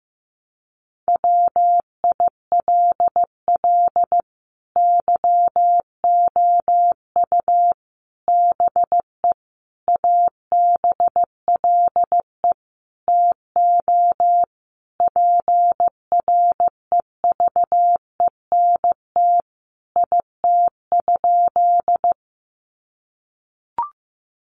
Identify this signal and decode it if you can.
Morse code, heard as a keyed tone, about 15 wpm, 700 Hz. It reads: WILL YOU BE ABLE TO PREVENT IT?
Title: Sentences from Top 1100 Words 15wpm